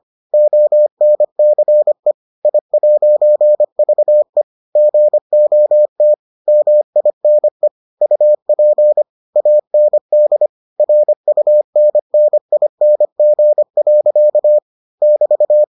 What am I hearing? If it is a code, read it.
Morse code, heard as a keyed tone, about 25 wpm, 600 Hz. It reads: ONCE I'VE GOT MINE UP AND RUNNING. <BT>